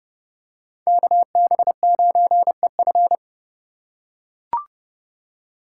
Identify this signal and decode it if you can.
Morse code, heard as a keyed tone, about 30 wpm, 700 Hz. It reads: KB9EF
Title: USA Call Signs - 5 Characters - 8x Character Spacing - 30wpm